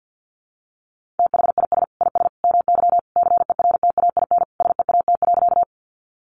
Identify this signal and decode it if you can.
Morse code, heard as a keyed tone, about 50 wpm, 700 Hz. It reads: THIS IS MY CERTAIN SEAT.
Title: Sentences from Top 1500 Words 50wpm